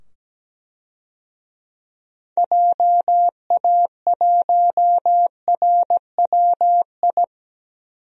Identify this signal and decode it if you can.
Morse code, heard as a keyed tone, about 17 wpm, 700 Hz. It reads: JA1RWI
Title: DXCC-100 Call Signs - 6 Characters - 17wpm